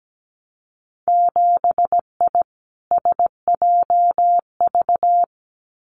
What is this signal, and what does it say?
Morse code, heard as a keyed tone, about 17 wpm, 700 Hz. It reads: 7I SJV
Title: Winter Field Day Exchanges Encore 17wpm